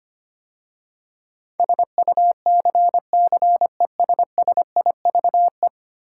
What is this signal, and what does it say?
Morse code, heard as a keyed tone, about 25 wpm, 690 Hz. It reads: SUCCESSIVE
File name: 10 Letter Words - First 3 Letters Given - 25wpm